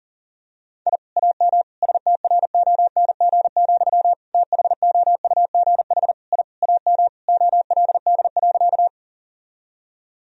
Morse code, heard as a keyed tone, about 40 wpm, 700 Hz. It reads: I AM STRONG, THOUGH I AM OLD.
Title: Sentences from Top 1000 Words - Review Part 3 - 40wpm